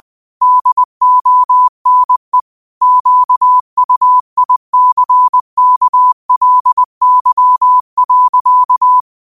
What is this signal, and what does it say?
Morse code, heard as a keyed tone, about 20 wpm, 1000 Hz. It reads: DONE QUICKLY.
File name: qotdfiles / 20wpm